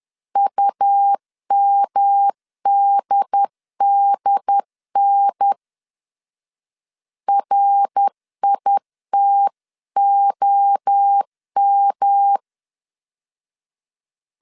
Morse code, heard as a keyed tone, about 10 wpm, 800 Hz. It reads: UMDDN RITOM
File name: CW / DET / 1 3 1500C S B 10wpm 3335